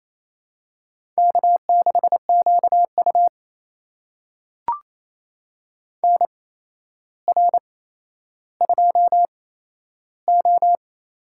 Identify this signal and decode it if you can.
Morse code, heard as a keyed tone, about 28 wpm, spaced as for 12 wpm, 700 Hz. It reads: K6QU NR2O
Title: USA Call Signs - 4 Characters - 8x Character Spacing - 28wpm